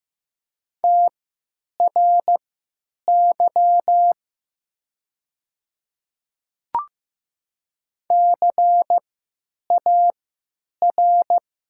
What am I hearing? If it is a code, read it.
Morse code, heard as a keyed tone, about 15 wpm, spaced as for 9 wpm, 700 Hz. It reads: TRY CAR
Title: 3x Character Spacing - Top 500 Words - 15wpm